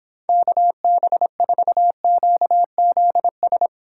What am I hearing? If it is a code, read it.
Morse code, heard as a keyed tone, about 26 wpm, 700 Hz. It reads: KB4QZS